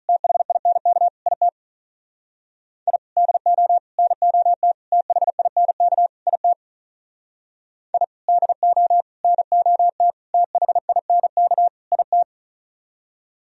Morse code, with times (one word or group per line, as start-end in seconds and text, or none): 0.1-1.1 THINK
1.3-1.5 IT
2.9-3.0 I
3.2-3.8 DO
4.0-4.7 NOT
4.9-6.1 THINK
6.3-6.5 IT
7.9-8.0 I
8.3-9.0 DO
9.2-10.1 NOT
10.3-11.7 THINK
11.9-12.2 IT